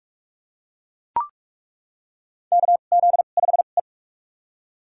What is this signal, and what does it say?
Morse code, heard as a keyed tone, about 43 wpm, spaced as for 32 wpm, 700 Hz. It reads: KZ5E
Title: USA Call Signs - 4 Characters - 2x Character Spacing - 45wpm